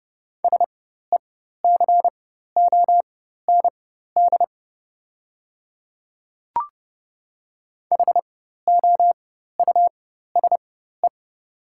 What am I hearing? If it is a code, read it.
Morse code, heard as a keyed tone, about 30 wpm, spaced as for 14 wpm, 700 Hz. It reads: SECOND HOUSE
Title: 4x Character Spacing - Top 300 Words - 30wpm